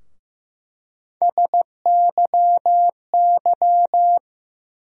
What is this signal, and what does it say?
Morse code, heard as a keyed tone, about 15 wpm, 700 Hz. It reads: SYY
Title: USA Call Sign Postfixes - 3 Letters 15wpm